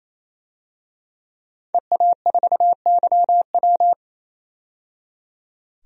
Morse code, heard as a keyed tone, about 28 wpm, 700 Hz. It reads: EA4YW
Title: DXCC-100 Call Signs - 5 Characters - 28wpm